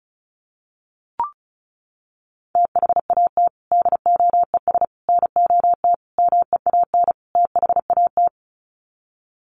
Morse code, heard as a keyed tone, about 34 wpm, 700 Hz. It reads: THAT DOES NOT MEAN THAT
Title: Sets of 5 Words Encore 35wpm